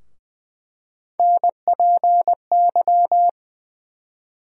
Morse code, heard as a keyed tone, about 21 wpm, 700 Hz. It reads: NPY
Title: USA Call Sign Postfixes - 3 Letters 20wpm